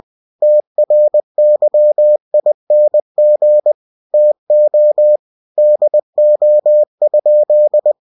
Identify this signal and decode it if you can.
Morse code, heard as a keyed tone, about 20 wpm, 600 Hz. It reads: TRYING TO DO?
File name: cw-books / winnie-the-pooh / winnie-20wpm-page-0048